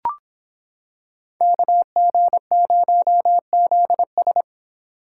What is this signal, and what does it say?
Morse code, heard as a keyed tone, about 26 wpm, 700 Hz. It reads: KG0ZS